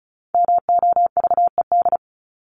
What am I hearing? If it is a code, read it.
Morse code, heard as a keyed tone, about 34 wpm, 700 Hz. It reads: MOVED
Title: ED Suffix - Top 100 Words with Base Words 35wpm